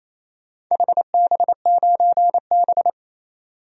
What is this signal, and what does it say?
Morse code, heard as a keyed tone, about 27 wpm, 700 Hz. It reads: HB9B
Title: DXCC-100 Call Signs - 4 Characters - 28wpm